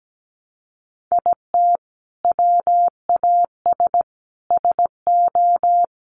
Morse code, heard as a keyed tone, about 17 wpm, 700 Hz. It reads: IT WAS SO
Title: Sentences from Top 500 Words 17wpm